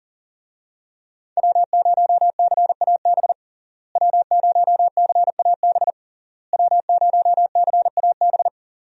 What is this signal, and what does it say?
Morse code, heard as a keyed tone, about 40 wpm, 700 Hz. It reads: W0CAB W0CAB W0CAB
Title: USA Call Signs - 5 Characters - Repeated 3x - 40wpm